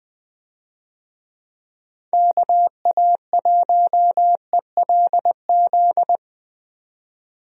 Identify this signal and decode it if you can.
Morse code, heard as a keyed tone, about 20 wpm, 700 Hz. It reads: KA1ELZ
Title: USA Call Signs - 6 Characters - 20wpm